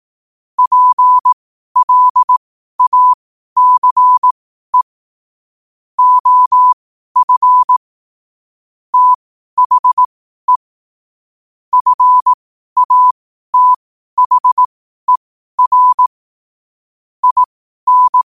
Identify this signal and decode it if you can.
Morse code, heard as a keyed tone, about 18 wpm, spaced as for 12 wpm, 1000 Hz. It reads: PLACE OF THE FATHER IN
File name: qotdfiles / 12wpm